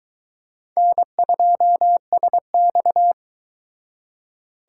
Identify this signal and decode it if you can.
Morse code, heard as a keyed tone, about 23 wpm, 700 Hz. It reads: N2SX